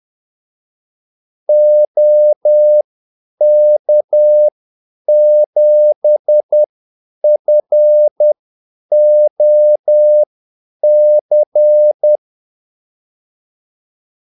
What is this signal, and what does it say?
Morse code, heard as a keyed tone, about 10 wpm, spaced as for 8 wpm, 600 Hz. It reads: OK7FOC
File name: Callsigns-10-8wpm